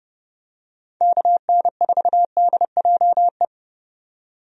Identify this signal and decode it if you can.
Morse code, heard as a keyed tone, about 31 wpm, 700 Hz. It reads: KN4DJE
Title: USA Call Signs - 6 Characters - 30wpm